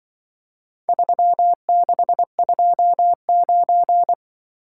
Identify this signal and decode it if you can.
Morse code, heard as a keyed tone, about 24 wpm, 700 Hz. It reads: 3629
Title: Speed-Racing - Four Letter-Number Combinations - 20wpm